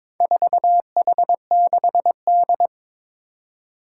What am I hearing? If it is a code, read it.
Morse code, heard as a keyed tone, about 22 wpm, 700 Hz. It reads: 4H6D
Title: Four Challenging Letter-Number Combinations 22wpm